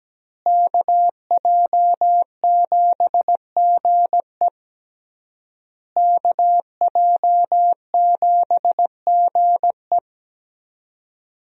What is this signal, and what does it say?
Morse code, heard as a keyed tone, about 17 wpm, 700 Hz. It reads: KJ7GE KJ7GE
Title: USA Call Signs - 5 Characters - Repeated 3x - 17wpm